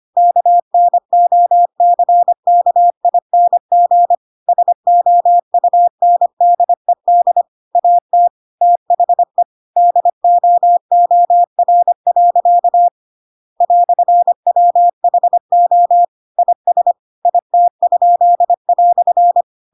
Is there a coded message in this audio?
Yes: Morse code from 0.2 to 19.4 s, heard as a keyed tone, about 25 wpm, 700 Hz. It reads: KNOCKING SOUNDED AT THE DOOR. "WHO IS IT?"